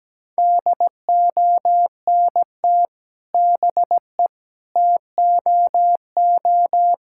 Morse code, heard as a keyed tone, about 17 wpm, 700 Hz. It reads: DONT BE TOO
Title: Sentences from Top 1000 Words - Review Part 5 - 17wpm